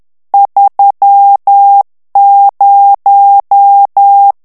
Morse code, heard as a keyed tone, about 10 wpm, 800 Hz. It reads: 30
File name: ANGLAIS texte clair 3 vitesse 10 mots minute